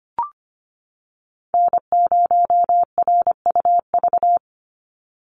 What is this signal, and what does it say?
Morse code, heard as a keyed tone, about 25 wpm, 700 Hz. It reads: N0RUV
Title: USA Call Signs 25wpm